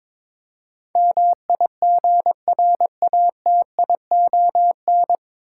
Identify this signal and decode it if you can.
Morse code, heard as a keyed tone, about 22 wpm, 700 Hz. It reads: MIGRATION
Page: Morse Code Ninja practice 22wpm